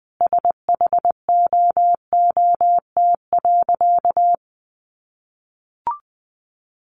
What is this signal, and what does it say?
Morse code, heard as a keyed tone, about 20 wpm, 700 Hz. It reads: SHOOT.
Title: Sentences from Top 1200 Words 20wpm